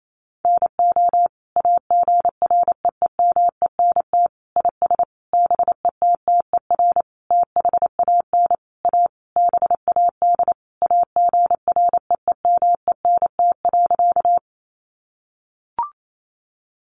Morse code, heard as a keyed tone, about 28 wpm, 700 Hz. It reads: NO AGREEMENT IS BETTER THAN A BAD AGREEMENT.
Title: Sentences from Top 1500 Words 28wpm